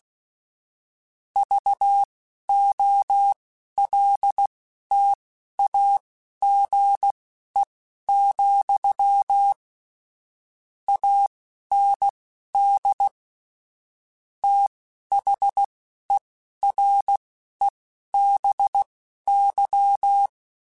Morse code, heard as a keyed tone, about 16 wpm, spaced as for 11 wpm, 800 Hz. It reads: VOLTAGE, AND THEREBY
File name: Technical Subjects / Technical Subjects-mp3s / XC @10